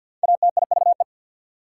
Morse code, heard as a keyed tone, about 63 wpm, 690 Hz. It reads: ATIVE